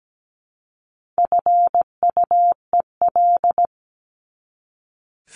Morse code, heard as a keyed tone, about 17 wpm, 700 Hz. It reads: FUEL
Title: Top 1751-2000 Words 17wpm